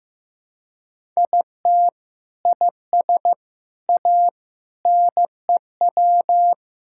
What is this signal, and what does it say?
Morse code, heard as a keyed tone, about 16 wpm, 700 Hz. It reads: IT IS A NEW
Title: Sentences from Top 1000 Words - Review Part 5 - 15wpm